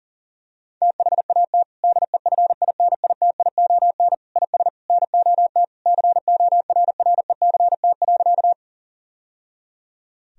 Morse code, heard as a keyed tone, about 40 wpm, 700 Hz. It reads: THAT DEFINITION IS NOT CORRECT.
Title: Sentences from Top 1000 Words - Review Part 4 - 40wpm